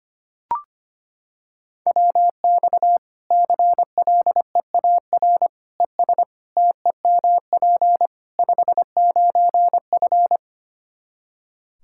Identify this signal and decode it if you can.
Morse code, heard as a keyed tone, about 25 wpm, 700 Hz. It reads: WX CLEAR ES TEMP 59F
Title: K1USN SST QSO Practice 25wpm